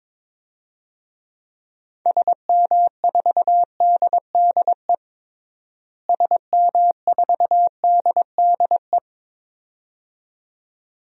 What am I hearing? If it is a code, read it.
Morse code, heard as a keyed tone, about 22 wpm, 700 Hz. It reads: SM4DDE SM4DDE
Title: DXCC-100 Call Signs - 6 Characters - Repeated 2x - 22wpm